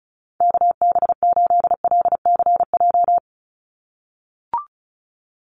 Morse code, heard as a keyed tone, about 34 wpm, 700 Hz. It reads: KB8LCJ
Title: USA Call Signs 35wpm